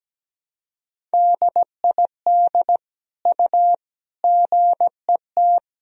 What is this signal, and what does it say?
Morse code, heard as a keyed tone, about 17 wpm, 700 Hz. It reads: DID U GET
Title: Sentences from Top 400 Words and Abbreviations 17wpm